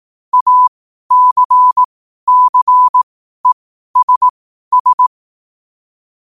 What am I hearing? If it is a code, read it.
Morse code, heard as a keyed tone, about 18 wpm, spaced as for 12 wpm, 1000 Hz. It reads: ACCESS
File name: qotdfiles / 12wpm